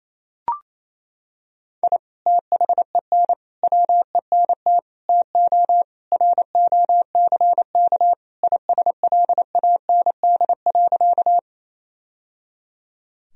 Morse code, heard as a keyed tone, about 28 wpm, 700 Hz. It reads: I THEN WENT TO ROCK ISLAND.
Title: Sentences from Top 1000 Words - Review Part 3 - 28wpm